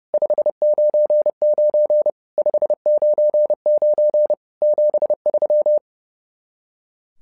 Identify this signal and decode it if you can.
Morse code, heard as a keyed tone, about 30 wpm, 600 Hz. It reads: 599 599 73
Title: SOTA QSO Practice 30wpm